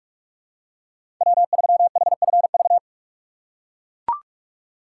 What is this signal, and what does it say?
Morse code, heard as a keyed tone, about 43 wpm, 700 Hz. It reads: W3HFV
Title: USA Call Signs - 5 Characters - 6x Character Spacing - 45wpm